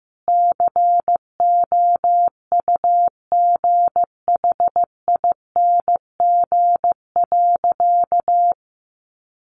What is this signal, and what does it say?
Morse code, heard as a keyed tone, about 15 wpm, 700 Hz. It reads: COUGHING.
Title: Additional Easy Phrases 15wpm